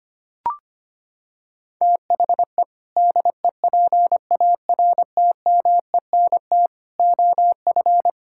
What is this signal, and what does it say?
Morse code, heard as a keyed tone, about 25 wpm, 700 Hz. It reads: THE DEPARTMENT OF
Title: Sentences from Top 1000 Words - Review Part 5 - 25wpm